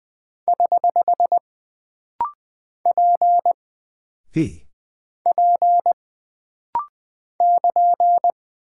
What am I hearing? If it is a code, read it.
Morse code, heard as a keyed tone, about 20 wpm, 700 Hz. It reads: <HH> P P <KN>
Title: Single Character (Letter:Number:Punctuation:Prosign) - Rapid-Fire - 20wpm